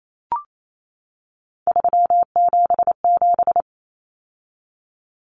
Morse code, heard as a keyed tone, about 28 wpm, 700 Hz. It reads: 377
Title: Common Cut Numbers - 3-Digit 28wpm